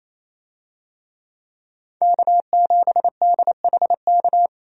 Morse code, heard as a keyed tone, about 28 wpm, 700 Hz. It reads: K7DHK
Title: USA Call Signs - 5 Characters - 28wpm